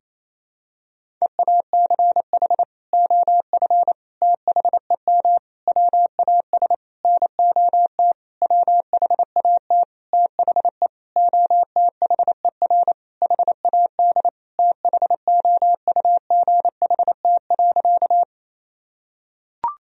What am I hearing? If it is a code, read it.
Morse code, heard as a keyed tone, about 28 wpm, 700 Hz. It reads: EACH OF THEM WAS NOT WHAT THE OTHER HAD THOUGHT.
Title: Sentences from Top 1000 Words - Review Part 1 - 28wpm